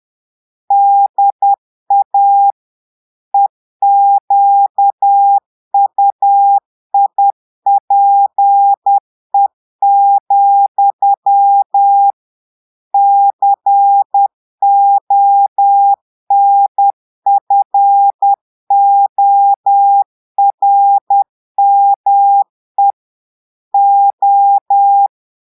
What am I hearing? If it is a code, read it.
Morse code, heard as a keyed tone, about 10 wpm, 800 Hz. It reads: DA EQUIPE, CONFORME O
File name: portugais texte en clair 2 vitesse 10 mots minutes